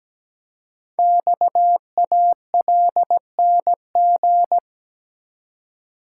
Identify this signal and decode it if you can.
Morse code, heard as a keyed tone, about 17 wpm, 700 Hz. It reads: XALNG
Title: Code Groups - 5 Letters - 17wpm